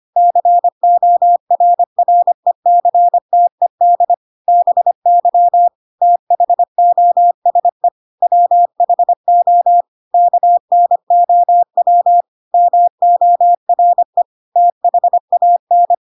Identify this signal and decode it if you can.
Morse code, heard as a keyed tone, about 25 wpm, 700 Hz. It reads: CORRECTED BY THOSE WHO KNOW MORE THAN
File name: Pinocchio / Pinocchio ch4 25 25 ik1hge